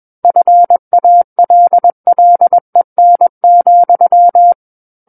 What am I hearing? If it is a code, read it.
Morse code, heard as a keyed tone, about 21 wpm, 700 Hz. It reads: FALLEN,
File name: Book One / 12-Chapter 12 - 21 WPM